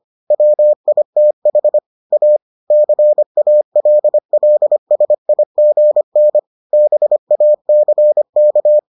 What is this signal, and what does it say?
Morse code, heard as a keyed tone, about 25 wpm, 600 Hz. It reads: WITH A CALLSIGN BACK